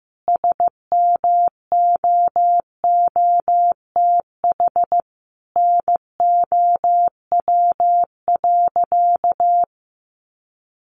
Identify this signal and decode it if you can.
Morse code, heard as a keyed tone, about 15 wpm, 700 Hz. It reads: SMOOTH NOW.
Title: Sentences from Top 1400 Words 15wpm